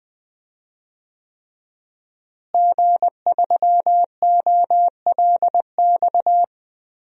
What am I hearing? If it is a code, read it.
Morse code, heard as a keyed tone, about 20 wpm, 700 Hz. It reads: G3OLX
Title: DXCC-100 Call Signs - 5 Characters - Repeated 3x - 20wpm